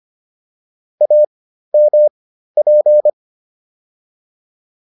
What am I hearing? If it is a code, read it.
Morse code, heard as a keyed tone, about 25 wpm, spaced as for 13 wpm, 600 Hz. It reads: AMP